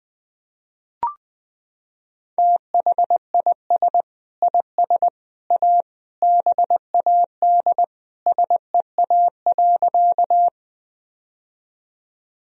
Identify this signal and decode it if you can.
Morse code, heard as a keyed tone, about 20 wpm, 700 Hz. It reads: THIS IS A BAD SEA.
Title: Sentences from Top 1000 Words - Review Part 1 - 20wpm